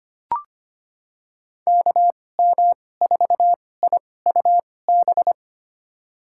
Morse code, heard as a keyed tone, about 25 wpm, spaced as for 18 wpm, 700 Hz. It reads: KM4IUB
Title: USA Call Signs - 6 Characters - 2x Character Spacing - 25wpm